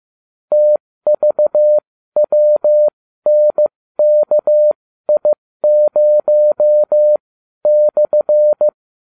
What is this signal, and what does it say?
Morse code, heard as a keyed tone, about 15 wpm, spaced as for 12 wpm, 600 Hz. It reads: TVWNKI0/